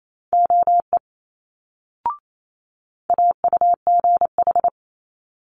Morse code, heard as a keyed tone, about 28 wpm, 700 Hz. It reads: OE AUGH
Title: Multi-letter Phonemes Quick Fire 28wpm